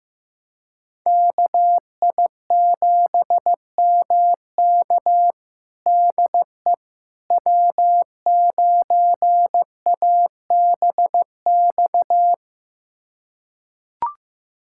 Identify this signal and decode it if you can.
Morse code, heard as a keyed tone, about 15 wpm, 700 Hz. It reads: KI7MK DE W9ABX
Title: USA Callsign DE Callsign - 5 Characters - 15wpm